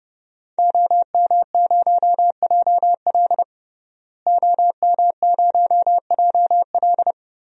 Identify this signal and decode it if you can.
Morse code, heard as a keyed tone, about 32 wpm, 700 Hz. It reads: OM0JL OM0JL
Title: DXCC-100 Call Signs - 5 Characters - Repeated 3x - 30wpm